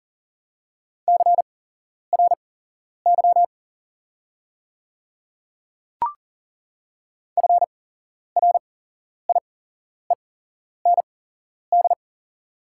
Morse code, heard as a keyed tone, about 39 wpm, spaced as for 11 wpm, 690 Hz. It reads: CRY FRIEND
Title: 8x Character Spacing - Top 500 Words - 40wpm